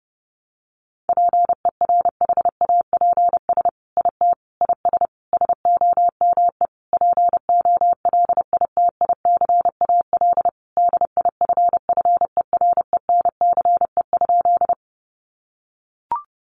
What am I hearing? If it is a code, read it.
Morse code, heard as a keyed tone, about 30 wpm, 700 Hz. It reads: PERHAPS IT IS SOME POLITICAL DIFFERENCE?
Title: Sentences from Top 800 Words 30wpm